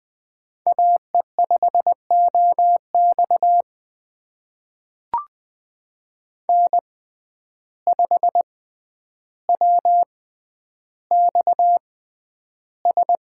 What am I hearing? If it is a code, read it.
Morse code, heard as a keyed tone, about 20 wpm, spaced as for 10 wpm, 700 Hz. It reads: AE5OX N5WXS
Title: USA Call Signs - 5 Characters - 6x Character Spacing - 20wpm